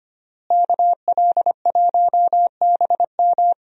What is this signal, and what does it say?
Morse code, heard as a keyed tone, about 25 wpm, 700 Hz. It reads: KL1BM
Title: USA Call Signs - 5 Characters - Repeated 3x - 25wpm